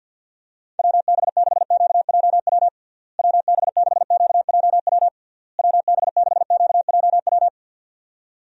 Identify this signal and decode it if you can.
Morse code, heard as a keyed tone, about 50 wpm, 700 Hz. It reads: WB6QJW WB6QJW WB6QJW
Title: USA Call Signs - 6 Characters - Repeated 3x - 50wpm